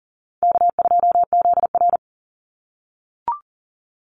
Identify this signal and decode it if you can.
Morse code, heard as a keyed tone, about 38 wpm, 700 Hz. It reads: K2ZR
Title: Call Signs - 40wpm